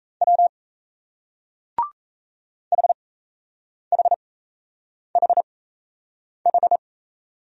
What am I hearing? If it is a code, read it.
Morse code, heard as a keyed tone, about 37 wpm, spaced as for 8 wpm, 700 Hz. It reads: W HHHH